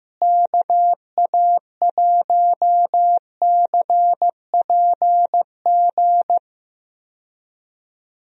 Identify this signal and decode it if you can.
Morse code, heard as a keyed tone, about 15 wpm, 700 Hz. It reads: KA1CPG
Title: USA Call Signs - 6 Characters - Repeated 2x - 15wpm